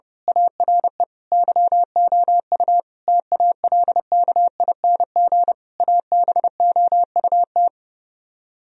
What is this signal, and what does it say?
Morse code, heard as a keyed tone, about 28 wpm, 700 Hz. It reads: ARE YOU TALKING ABOUT
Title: Speed-Racing - Sets of 5 Words 30wpm